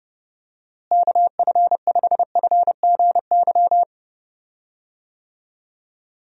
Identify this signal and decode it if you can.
Morse code, heard as a keyed tone, about 30 wpm, 700 Hz. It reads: KF5FGY
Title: USA Call Signs - 6 Characters - Repeated 2x - 30wpm